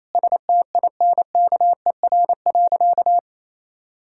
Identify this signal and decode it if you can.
Morse code, heard as a keyed tone, about 28 wpm, 700 Hz. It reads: STINKER.
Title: Pangram Challenge v2 - 28wpm